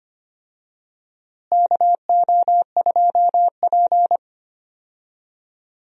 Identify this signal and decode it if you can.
Morse code, heard as a keyed tone, about 25 wpm, 700 Hz. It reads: KO2P
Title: USA Call Signs - 4 Characters - Repeated 2x - 25wpm